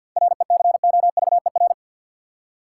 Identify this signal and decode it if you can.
Morse code, heard as a keyed tone, about 53 wpm, 700 Hz. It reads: RECOVER
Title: Seven Letter Words - 50wpm